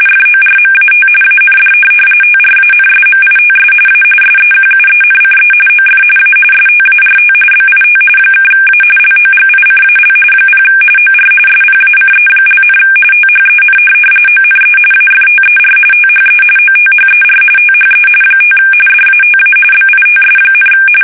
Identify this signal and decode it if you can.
RTTY, about 45.45 baud. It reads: FORMAT AND TRANSMITS A SEQUENCE OF A START BIT (A LOGICAL 0 OR SPACE), THEN ONE AFTER THE OTHER THE 5 DATA BITS,
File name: fsk850 / teletype-RTTY